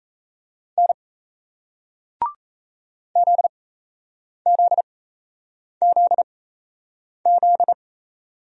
Morse code, heard as a keyed tone, about 34 wpm, spaced as for 8 wpm, 700 Hz. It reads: N ZZZZ
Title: Speed-Racing Single Letters 28wpm